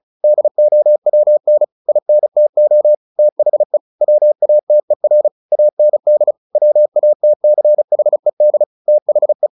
Morse code, heard as a keyed tone, about 34 wpm, 600 Hz. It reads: DOWN INTO THE WATER AND WATCHED THE
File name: the old man and the sea / the old man and the sea 1 35wpm